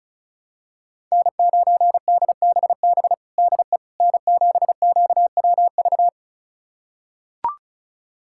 Morse code, heard as a keyed tone, about 36 wpm, 700 Hz. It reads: N9DBB DE N7QWV
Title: USA Callsign DE Callsign - 5 Characters - 35wpm